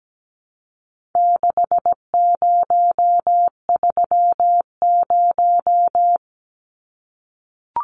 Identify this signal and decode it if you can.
Morse code, heard as a keyed tone, about 17 wpm, 700 Hz. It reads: 6030
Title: Numbers - 4-Digit 17wpm